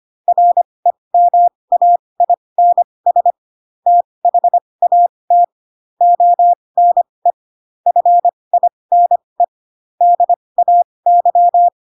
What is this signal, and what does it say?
Morse code, heard as a keyed tone, about 25 wpm, spaced as for 20 wpm, 700 Hz. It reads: REMAINS THAT ONE FINE DAY
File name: Pinocchio / Pinocchio ch1 15 25 ik1hge